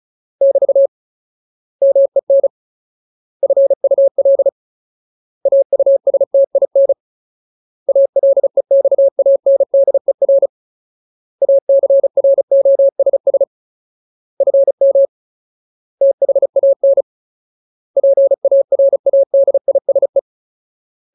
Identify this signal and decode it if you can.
Morse code, heard as a keyed tone, about 34 wpm, 550 Hz. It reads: <BT> MEN FUL AUSTIN ALEXANDER ACROSS FM THAN PARADISE
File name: Ultimate Word List / 35WPM Ultimate